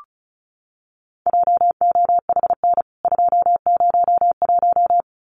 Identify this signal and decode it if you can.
Morse code, heard as a keyed tone, about 34 wpm, 700 Hz. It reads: JOHN 201
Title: CWT Exchange 35wpm